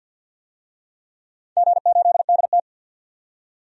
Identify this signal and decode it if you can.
Morse code, heard as a keyed tone, about 50 wpm, 700 Hz. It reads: G8DT
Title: DXCC-100 Call Signs - 4 Characters - 50wpm